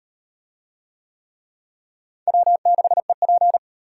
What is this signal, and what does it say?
Morse code, heard as a keyed tone, about 38 wpm, 700 Hz. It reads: W6EP